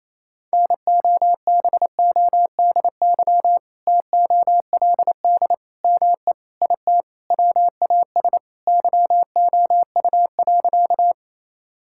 Morse code, heard as a keyed tone, about 27 wpm, 700 Hz. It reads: NOBODY TOLD ME IT WAS YOU.
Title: Sentences from Top 1600 Words 28wpm